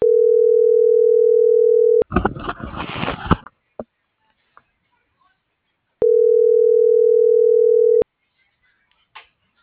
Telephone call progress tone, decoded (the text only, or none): ringback tone